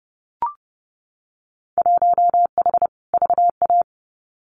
Morse code, heard as a keyed tone, about 30 wpm, 700 Hz. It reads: 1H VA
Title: Winter Field Day Exchanges 30wpm